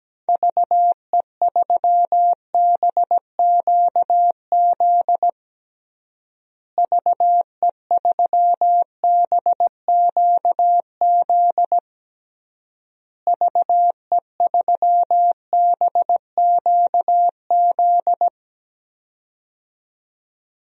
Morse code, heard as a keyed tone, about 17 wpm, 700 Hz. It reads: VE3BQZ VE3BQZ VE3BQZ
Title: DXCC-100 Call Signs - 6 Characters - Repeated 3x - 17wpm